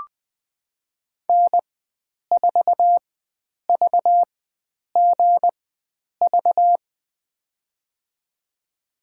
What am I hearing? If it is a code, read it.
Morse code, heard as a keyed tone, about 21 wpm, spaced as for 9 wpm, 700 Hz. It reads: N4VGV
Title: USA Call Signs - 5 Characters - 4x Character Spacing - 20wpm